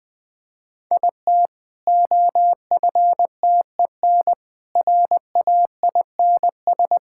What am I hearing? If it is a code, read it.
Morse code, heard as a keyed tone, about 20 wpm, 700 Hz. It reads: IT OFTEN RAINS
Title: Easy Sentences with Hints - 20wpm